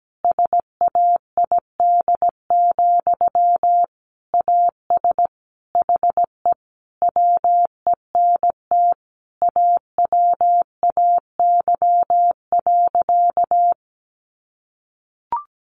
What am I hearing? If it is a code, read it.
Morse code, heard as a keyed tone, about 17 wpm, 700 Hz. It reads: SAID, AS HE WENT AWAY.
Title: Sentences from Top 600 Words 17wpm